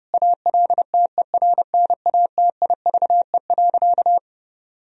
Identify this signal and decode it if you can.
Morse code, heard as a keyed tone, about 30 wpm, 700 Hz. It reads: ALTERNATIVE.
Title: Sentences from Top 1000 Words - Review Part 4 - 30wpm